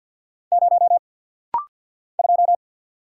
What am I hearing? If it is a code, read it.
Morse code, heard as a keyed tone, about 50 wpm, 700 Hz. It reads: 0 2
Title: Single Letter-Number - Rapid-Fire 50wpm